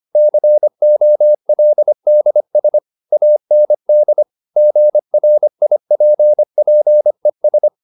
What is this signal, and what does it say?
Morse code, heard as a keyed tone, about 25 wpm, 600 Hz. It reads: COLDS AND GRIPPES
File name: the old man and the sea part 1 25wpm